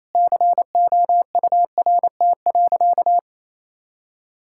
Morse code, heard as a keyed tone, about 28 wpm, 700 Hz. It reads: COURT.
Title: Sentences from Top 1600 Words 28wpm